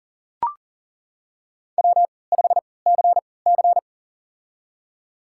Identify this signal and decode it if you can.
Morse code, heard as a keyed tone, about 40 wpm, spaced as for 23 wpm, 700 Hz. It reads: W5CC
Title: USA Call Signs - 4 Characters - 3x Character Spacing - 40wpm